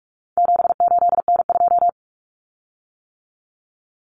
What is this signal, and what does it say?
Morse code, heard as a keyed tone, about 44 wpm, 700 Hz. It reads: 78N2
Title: Common Cut Numbers - 4-Digit 45wpm